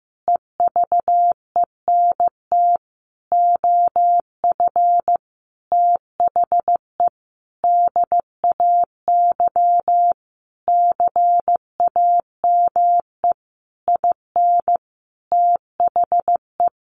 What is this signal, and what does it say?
Morse code, heard as a keyed tone, about 15 wpm, 700 Hz. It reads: EVENT OF THE DAY CAME IN THE
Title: Sentences from Top 1500 Words 15wpm